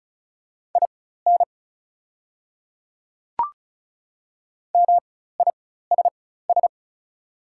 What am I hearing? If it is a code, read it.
Morse code, heard as a keyed tone, about 33 wpm, spaced as for 16 wpm, 690 Hz. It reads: IN MISS
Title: 4x Character Spacing - Top 500 Words - 35wpm